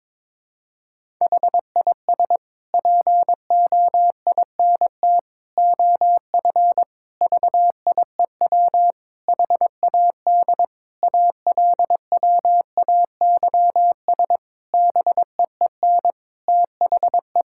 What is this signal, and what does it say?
Morse code, heard as a keyed tone, about 22 wpm, 700 Hz. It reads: HIS POINT OF VIEW HAD ALWAYS BEEN THE
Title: Sentences from Top 1000 Words - Review Part 6 - 22wpm